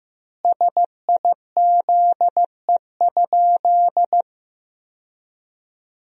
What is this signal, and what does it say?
Morse code, heard as a keyed tone, about 15 wpm, 700 Hz. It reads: SIZE?
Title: 3x Word-Spacing - Sentences from Top 500 Words - 15wpm